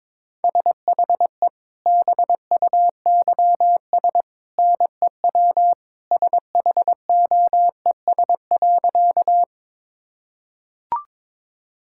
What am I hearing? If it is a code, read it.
Morse code, heard as a keyed tone, about 22 wpm, 700 Hz. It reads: SHE BUYS NEW SHOES.
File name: practice / 22wpm / Easy Sentences with Hints - 22wpm